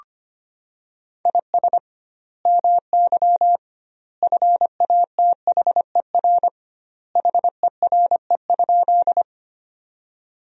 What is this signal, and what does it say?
Morse code, heard as a keyed tone, about 25 wpm, 700 Hz. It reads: IS MY FATHER HERE?
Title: 2x Word-Spacing - Sentences from Top 300 Words - 25wpm